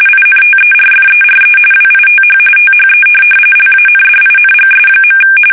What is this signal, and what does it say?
RTTY, about 45.45 baud. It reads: INTERFERE WITH THE LANDINGS.
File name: fsk850 / USS-Massachusetts